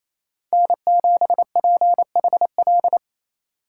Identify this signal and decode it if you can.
Morse code, heard as a keyed tone, about 36 wpm, 700 Hz. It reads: N7PHL